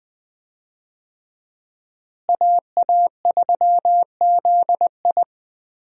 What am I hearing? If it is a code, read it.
Morse code, heard as a keyed tone, about 20 wpm, 700 Hz. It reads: AA3ZI
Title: USA Call Signs - 5 Characters - Repeated 3x - 20wpm